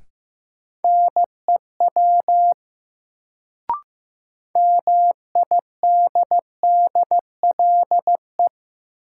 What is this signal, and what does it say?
Morse code, heard as a keyed tone, about 15 wpm, 700 Hz. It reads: NEW MIDDLE
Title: Sets of 2 Words - New Word Drill - 15wpm